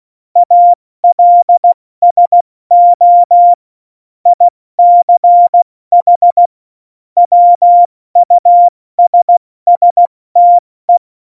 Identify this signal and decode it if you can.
Morse code, heard as a keyed tone, about 16 wpm, 700 Hz. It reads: ALSO ICH WUSSTE
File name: Morsekurs / Witze / Witz 09. Juli 2019, 70 F80